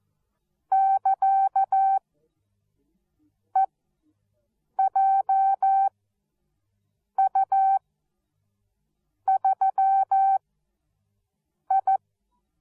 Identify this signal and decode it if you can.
Morse code, heard as a keyed tone, about 14 wpm, spaced as for 5 wpm, 770 Hz. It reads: <KA>EJU3I
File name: Onlinelehrgang / morsen / DJ4UF Morsen13